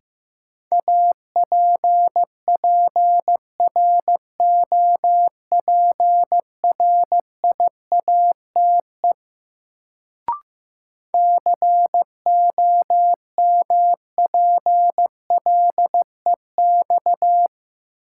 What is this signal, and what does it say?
Morse code, heard as a keyed tone, about 15 wpm, 700 Hz. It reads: APPROPRIATE COMPLEX
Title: Sentences from Top 900 Words - New Word Drill - 15wpm